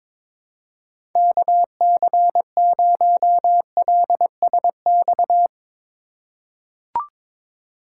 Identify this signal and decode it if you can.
Morse code, heard as a keyed tone, about 22 wpm, 700 Hz. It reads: KC0LSX
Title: USA Call Signs - 6 Characters - 2x Character Spacing - 22wpm